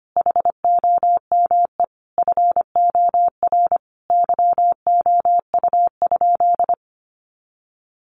Morse code, heard as a keyed tone, about 25 wpm, 700 Hz. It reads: HOME FOR YOU?
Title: Sentences from Top 1400 Words 25wpm